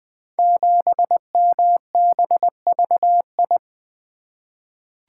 Morse code, heard as a keyed tone, about 20 wpm, 700 Hz. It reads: 7MBVI
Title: Code Groups - 5 Letter:Numbers - 20wpm